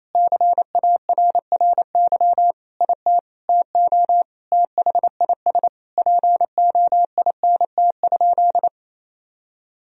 Morse code, heard as a keyed tone, about 28 wpm, 700 Hz. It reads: CARRY IT TO THIS POINT?
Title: Sentences from Top 1000 Words - Review Part 1 - 28wpm